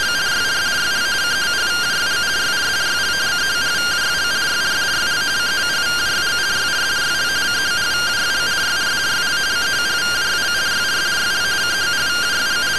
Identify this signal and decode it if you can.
Two-tone teleprinter signal, about 170 Hz shift.
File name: findings / MR CORONAL 55.gen-00 rtty rx